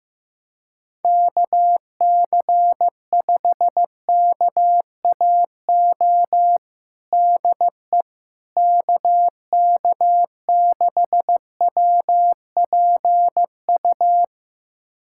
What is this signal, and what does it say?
Morse code, heard as a keyed tone, about 15 wpm, 700 Hz. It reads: KC5KAO DE KK6WPU
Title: USA Callsign DE Callsign 15wpm